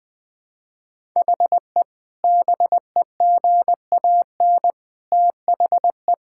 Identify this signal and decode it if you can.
Morse code, heard as a keyed tone, about 20 wpm, 700 Hz. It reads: HE BEGAN THE
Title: Sentences from Top 1000 Words - Review Part 5 - 20wpm